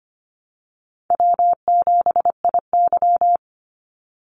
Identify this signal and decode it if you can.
Morse code, heard as a keyed tone, about 25 wpm, 700 Hz. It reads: W7IY
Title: Easy Call Signs Repeated 3x - 25wpm